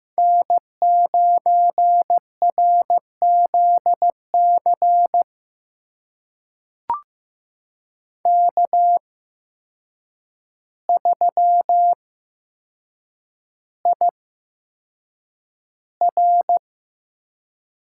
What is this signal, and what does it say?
Morse code, heard as a keyed tone, about 15 wpm, spaced as for 7 wpm, 700 Hz. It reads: N9RZC K3IR
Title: USA Call Signs - 5 Characters - 8x Character Spacing - 15wpm